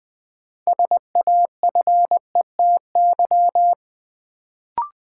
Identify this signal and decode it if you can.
Morse code, heard as a keyed tone, about 20 wpm, 700 Hz. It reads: SAFETY
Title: Sentences from Top 700 Words - New Word Drill - 20wpm